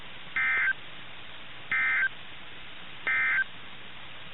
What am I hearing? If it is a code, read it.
Emergency Alert System (SAME) tones, heard as harsh buzzing bursts. It is heard from 0.2 to 3.5 s.
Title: NOAA Weather Radio - KIH61